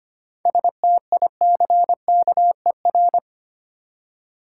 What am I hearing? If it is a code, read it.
Morse code, heard as a keyed tone, about 25 wpm, 690 Hz. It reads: STICKER
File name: practice / 25wpm / Easy Words with Hints - 25wpm